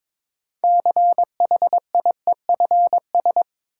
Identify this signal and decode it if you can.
Morse code, heard as a keyed tone, about 22 wpm, 700 Hz. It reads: CHIEFS
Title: Sets of 5 Words Encore - New Word Drill - 22wpm